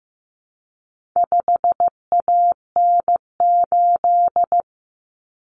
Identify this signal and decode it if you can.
Morse code, heard as a keyed tone, about 15 wpm, 700 Hz. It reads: 5AN8
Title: Common Cut Numbers - 4-Digit 15wpm